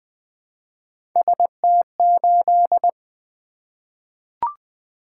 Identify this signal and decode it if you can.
Morse code, heard as a keyed tone, about 20 wpm, 700 Hz. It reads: ST8
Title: Three Letter-Number Combinations - 20wpm